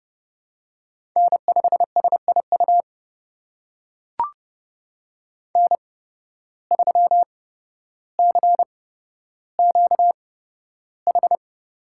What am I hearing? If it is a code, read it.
Morse code, heard as a keyed tone, about 28 wpm, spaced as for 13 wpm, 700 Hz. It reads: N5SIU N3CQH
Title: USA Call Signs - 5 Characters - 8x Character Spacing - 30wpm